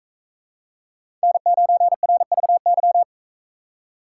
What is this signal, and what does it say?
Morse code, heard as a keyed tone, about 40 wpm, 700 Hz. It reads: N9RVY